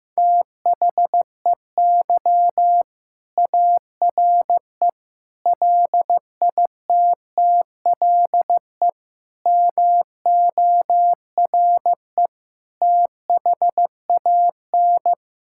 Morse code, heard as a keyed tone, about 15 wpm, 700 Hz. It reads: THEY ARE LITTLE MORE THAN
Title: Sentences from Top 1600 Words 15wpm